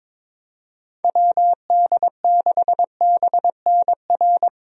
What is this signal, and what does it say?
Morse code, heard as a keyed tone, about 22 wpm, 700 Hz. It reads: WD6BNR
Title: USA Call Signs - 6 Characters - Repeated 2x - 22wpm